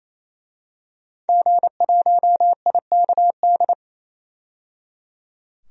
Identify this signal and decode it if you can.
Morse code, heard as a keyed tone, about 28 wpm, 700 Hz. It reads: G1IKD
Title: DXCC-100 Call Signs - 5 Characters - 28wpm